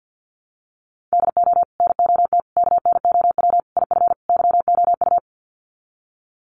Morse code, heard as a keyed tone, about 50 wpm, 700 Hz. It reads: DO NOT KNOW IF YOU
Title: Sets of 5 Words 50wpm